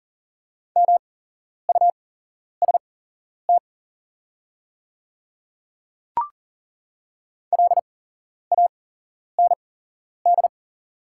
Morse code, heard as a keyed tone, about 40 wpm, spaced as for 11 wpm, 700 Hz. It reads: MUST LAND
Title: 8x Character Spacing - Top 200 Words - 40wpm